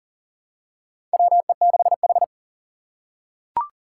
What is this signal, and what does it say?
Morse code, heard as a keyed tone, about 37 wpm, 700 Hz. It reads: WE6H
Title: USA Call Signs - 4 Characters - 8x Character Spacing - 40wpm